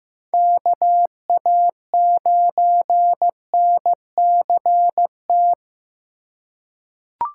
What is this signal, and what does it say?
Morse code, heard as a keyed tone, about 15 wpm, 700 Hz. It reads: KA9NCT
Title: USA Call Signs - 6 Characters - 3x Character Spacing - 15wpm